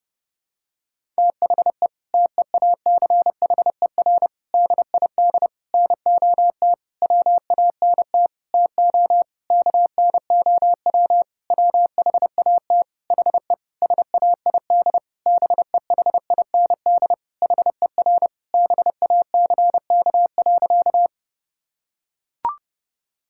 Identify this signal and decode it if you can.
Morse code, heard as a keyed tone, about 28 wpm, 700 Hz. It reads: THE TEACHER DID NOT WANT TO KNOW WHAT HE SAID BEHIND HER BACK.
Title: Sentences from Top 1000 Words - Review Part 6 - 30wpm